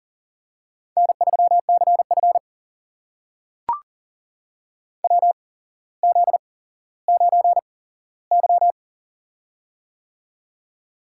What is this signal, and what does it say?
Morse code, heard as a keyed tone, about 39 wpm, spaced as for 17 wpm, 700 Hz. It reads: N3CF WZ9Y
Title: USA Call Signs - 4 Characters - 8x Character Spacing - 40wpm